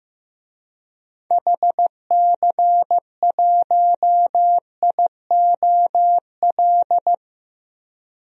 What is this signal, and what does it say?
Morse code, heard as a keyed tone, about 15 wpm, 700 Hz. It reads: HC1IOL
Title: DXCC-100 Call Signs - 6 Characters - 15wpm